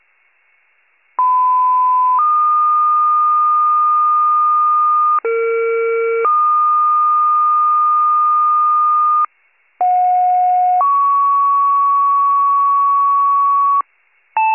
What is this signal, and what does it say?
Two-tone page, heard as a one-second tone, then a three-second tone, about 1010 then 1220 Hz, about 460 then 1220 Hz, about 730 then 1120 Hz. It is heard from 1.2 to 13.8 s.